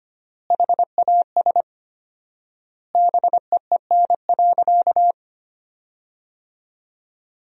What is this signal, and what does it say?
Morse code, heard as a keyed tone, about 25 wpm, 700 Hz. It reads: HAS BEEN.
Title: 4x Word-Spacing - Sentences from Top 300 Words - 25wpm